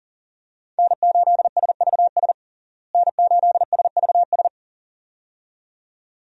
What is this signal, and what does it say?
Morse code, heard as a keyed tone, about 40 wpm, 700 Hz. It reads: N8SVS N8SVS
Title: USA Call Signs - 5 Characters - Repeated 2x - 40wpm